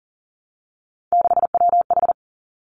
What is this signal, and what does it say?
Morse code, heard as a keyed tone, about 38 wpm, 690 Hz. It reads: 6WH
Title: Three Challenging Letter-Number Combinations 40wpm